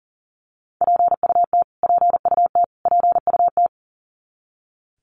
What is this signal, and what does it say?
Morse code, heard as a keyed tone, about 40 wpm, 700 Hz. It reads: PUT PUT PUT
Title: Top 300 Words Repeated 40wpm